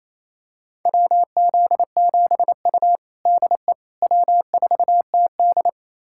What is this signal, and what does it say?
Morse code, heard as a keyed tone, about 28 wpm, 700 Hz. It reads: WZ7U DE W4TD
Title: USA Callsign DE Callsign - 4 Characters - 28wpm